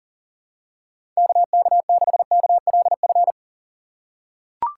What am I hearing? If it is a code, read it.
Morse code, heard as a keyed tone, about 40 wpm, 690 Hz. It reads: KK6KLF
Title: USA Call Signs - 6 Characters - 2x Character Spacing - 40wpm